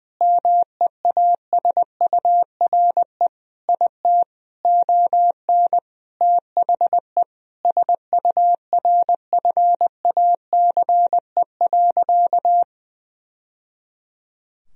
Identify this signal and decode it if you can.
Morse code, heard as a keyed tone, about 20 wpm, 700 Hz. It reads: MEASURE IT ON THE SURFACE.
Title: Sentences from Top 1000 Words - Review Part 3 - 20wpm